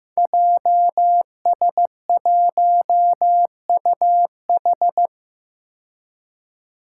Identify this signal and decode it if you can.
Morse code, heard as a keyed tone, about 15 wpm, 700 Hz. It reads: JS1UH
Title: Five Challenging Letter-Number Combinations 15wpm